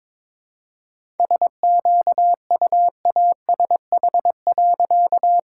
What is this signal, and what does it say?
Morse code, heard as a keyed tone, about 21 wpm, 700 Hz. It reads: SQUASH.
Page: morse-code-ninja-practice-22wpm